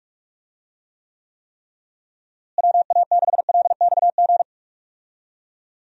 Morse code, heard as a keyed tone, about 46 wpm, 700 Hz. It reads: WA6LXG
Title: USA Call Signs - 6 Characters - Repeated 3x - 45wpm